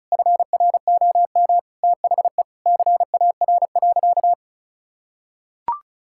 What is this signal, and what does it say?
Morse code, heard as a keyed tone, about 36 wpm, 700 Hz. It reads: FROM THE CAR.
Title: Sentences from Top 1000 Words - Review Part 5 - 35wpm